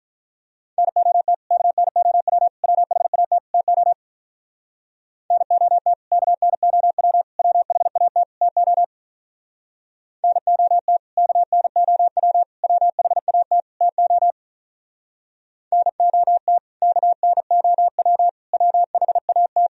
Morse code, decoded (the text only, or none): NOT KNOW WHAT TO NOT KNOW WHAT TO NOT KNOW WHAT TO NOT KNOW WHAT